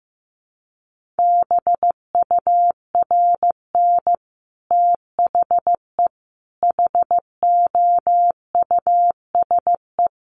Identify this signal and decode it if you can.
Morse code, heard as a keyed tone, about 15 wpm, 700 Hz. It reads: BURN THE HOUSE
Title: Sentences from Top 1300 Words 15wpm